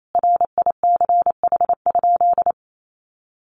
Morse code, heard as a keyed tone, about 28 wpm, 700 Hz. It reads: RICH?
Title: Sentences from Top 1100 Words 28wpm